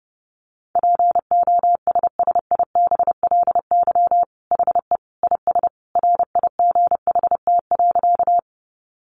Morse code, heard as a keyed tone, about 30 wpm, 700 Hz. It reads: POSSIBLY HE IS RIGHT.
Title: Sentences from Top 1300 Words 30wpm